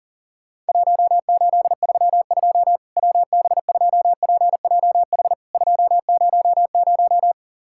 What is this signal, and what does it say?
Morse code, heard as a keyed tone, about 40 wpm, 700 Hz. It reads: 1832 WB2PJH 200
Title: Worked All Europe DX Contest - QTC Exchange - 40wpm